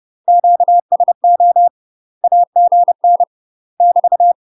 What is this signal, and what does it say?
Morse code, heard as a keyed tone, about 30 wpm, 700 Hz. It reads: QSO AGN <BT>
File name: UK MASTER QSOs TRAINING / 30WPM / UK MASTER QSOs 0027